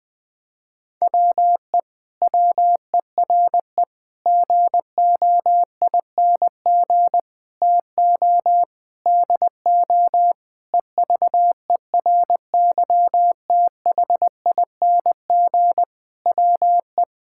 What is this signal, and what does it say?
Morse code, heard as a keyed tone, about 20 wpm, 700 Hz. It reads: WE WERE GOING TO DO EVERYTHING WE
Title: Sentences with 8-Word Phrases - Spoken Beginning - 20wpm